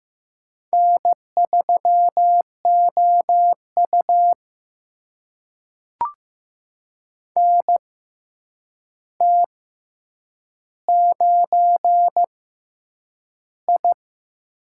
Morse code, heard as a keyed tone, about 15 wpm, spaced as for 8 wpm, 700 Hz. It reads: N3OU NT9I
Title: USA Call Signs - 4 Characters - 6x Character Spacing - 15wpm